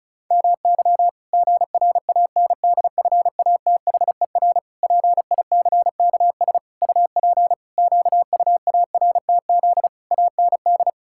Morse code, heard as a keyed tone, about 34 wpm, 700 Hz. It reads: MY GRANDFATHER PICKS UP QUARTZ AND
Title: Pangram Challenge v2 - 35wpm